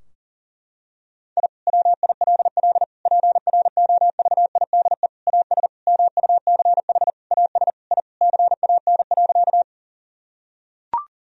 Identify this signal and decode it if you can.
Morse code, heard as a keyed tone, about 40 wpm, 700 Hz. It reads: I WILL PROVIDE AS MUCH AS I CAN.
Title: Sentences from Top 1000 Words - Review Part 3 - 40wpm